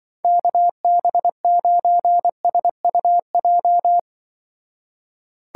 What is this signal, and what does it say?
Morse code, heard as a keyed tone, about 24 wpm, 700 Hz. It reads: KB9SUJ
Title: Speed Racing - USA 2x3 Call Signs 20wpm